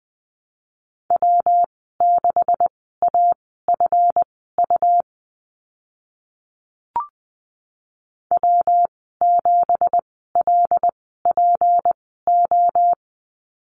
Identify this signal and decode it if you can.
Morse code, heard as a keyed tone, about 20 wpm, spaced as for 15 wpm, 700 Hz. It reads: W6AFU W7LPO
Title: 186 Lesson 30g - Letter b - Call Signs 20wpm